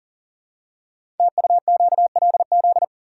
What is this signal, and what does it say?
Morse code, heard as a keyed tone, about 40 wpm, 690 Hz. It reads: TUQLZ
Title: Code Groups - 5 Letters - 40wpm